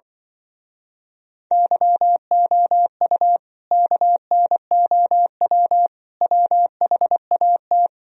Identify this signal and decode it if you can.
Morse code, heard as a keyed tone, about 24 wpm, 700 Hz. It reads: YOU KNOW WHAT
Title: Speed-Racing - Sets of 3 Words 20wpm